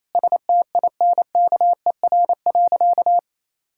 Morse code, heard as a keyed tone, about 28 wpm, 700 Hz. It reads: STINKER.
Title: Pangram Challenge v2 - 28wpm